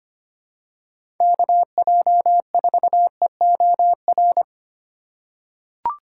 Morse code, heard as a keyed tone, about 25 wpm, 700 Hz. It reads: KJ4EOR
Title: USA Call Signs - 6 Characters - 4x Character Spacing - 25wpm